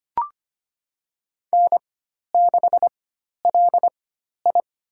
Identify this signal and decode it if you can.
Morse code, heard as a keyed tone, about 25 wpm, spaced as for 12 wpm, 700 Hz. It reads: N6LI
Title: USA Call Signs - 4 Characters - 4x Character Spacing - 25wpm